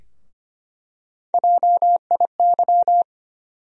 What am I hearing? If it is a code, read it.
Morse code, heard as a keyed tone, about 25 wpm, 700 Hz. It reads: JIY
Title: USA Call Sign Postfixes - 3 Letters 25wpm